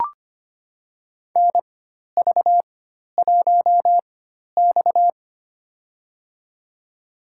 Morse code, heard as a keyed tone, about 25 wpm, spaced as for 12 wpm, 700 Hz. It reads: NV1X
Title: USA Call Signs - 4 Characters - 4x Character Spacing - 25wpm